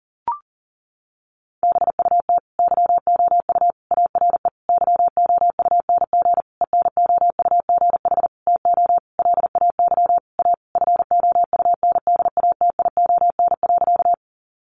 Morse code, heard as a keyed tone, about 40 wpm, 700 Hz. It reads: BUT YOU ARE YOUNG ENOUGH TO LAY A FOUNDATION.
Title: Sentences from Top 1200 Words 40wpm